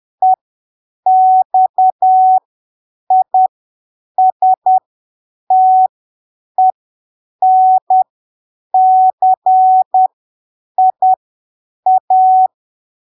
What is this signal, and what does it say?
Morse code, heard as a keyed tone, about 10 wpm, spaced as for 7 wpm, 750 Hz. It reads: EXISTENCIA
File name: arquivos / texto / texto 29 10ppm